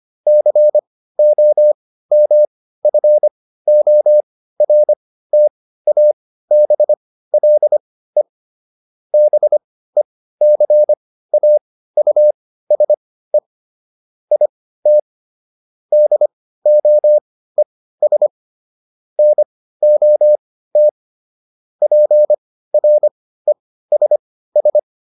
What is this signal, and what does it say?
Morse code, heard as a keyed tone, about 25 wpm, spaced as for 15 wpm, 600 Hz. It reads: COMFORTABLE BECAUSE IT DOES NOT PRESS